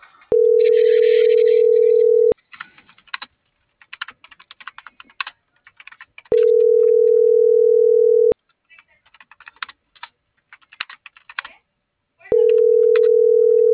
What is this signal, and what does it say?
Ringback tone (North America), 440 + 480 Hz.
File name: CENTRAL MAYAB / out brandon.fi